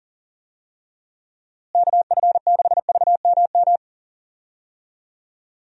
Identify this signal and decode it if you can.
Morse code, heard as a keyed tone, about 40 wpm, 700 Hz. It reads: KF6VMM
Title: USA Call Signs - 6 Characters - Repeated 3x - 40wpm